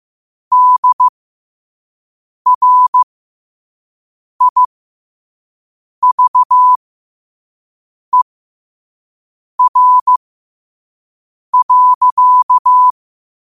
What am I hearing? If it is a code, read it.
Morse code, heard as a keyed tone, about 15 wpm, spaced as for 5 wpm, 1000 Hz. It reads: DRIVER.
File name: qotdfiles / 5wpm